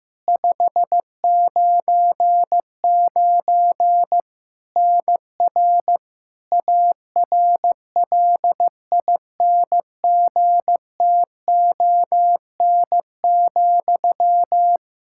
Morse code, heard as a keyed tone, about 15 wpm, 700 Hz. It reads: 599 NR ARLINGTON,
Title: Practice QSO - Exchange 1 - Beginning - 15wpm